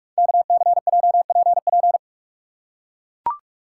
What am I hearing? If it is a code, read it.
Morse code, heard as a keyed tone, about 44 wpm, 700 Hz. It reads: KCJPP
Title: Code Groups - 5 Letters - 45wpm